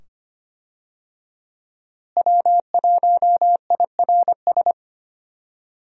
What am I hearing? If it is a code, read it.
Morse code, heard as a keyed tone, about 25 wpm, 700 Hz. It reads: W1IRS
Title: USA Call Signs - 5 Characters - Repeated 2x - 25wpm